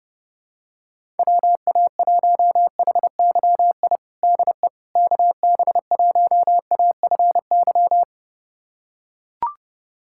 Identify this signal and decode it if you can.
Morse code, heard as a keyed tone, about 31 wpm, 700 Hz. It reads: WA1HYI DE KB1AFY
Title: USA Callsign DE Callsign - 6 Characters - 30wpm